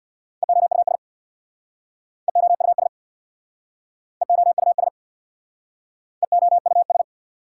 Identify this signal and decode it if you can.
Morse code, heard as a keyed tone, about 66 wpm, 700 Hz. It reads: EOUS EOUS EOUS EOUS